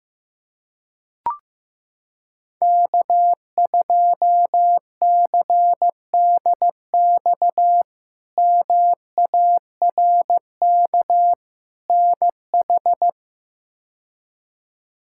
Morse code, heard as a keyed tone, about 15 wpm, 700 Hz. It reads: K2CDX MARK NH
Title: K1USN SST QSO Practice - 15wpm